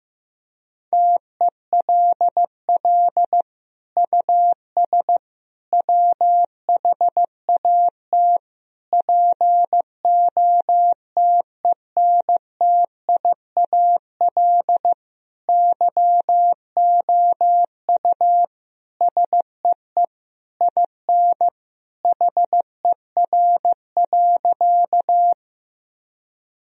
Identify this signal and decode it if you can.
Morse code, heard as a keyed tone, about 15 wpm, 700 Hz. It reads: TELL US WHAT POTENTIAL YOU SEE IN HER.
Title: Sentences from Top 1000 Words - Review Part 4 - 15wpm